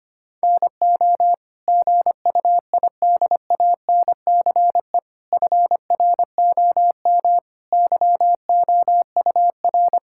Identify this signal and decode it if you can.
Morse code, heard as a keyed tone, about 25 wpm, 700 Hz. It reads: NO GUIDANCE FROM YOUR
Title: Sentences from Top 1600 Words 25wpm